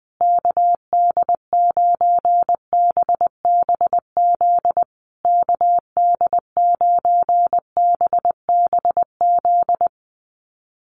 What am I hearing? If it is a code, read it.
Morse code, heard as a keyed tone, about 20 wpm, 700 Hz. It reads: KD9BBZ KD9BBZ
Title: USA Call Signs Repeated 3x 20wpm